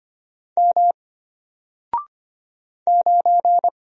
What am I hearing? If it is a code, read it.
Morse code, heard as a keyed tone, about 25 wpm, 700 Hz. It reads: M 9
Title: Single Letter-Number Alt 25wpm